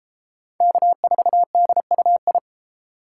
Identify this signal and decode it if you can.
Morse code, heard as a keyed tone, about 32 wpm, 700 Hz. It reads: K4DUI